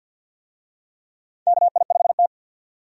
Morse code, heard as a keyed tone, about 46 wpm, 700 Hz. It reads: KI5T
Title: USA Call Signs - 4 Characters - 50wpm